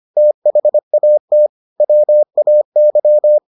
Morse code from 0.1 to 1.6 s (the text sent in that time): THAT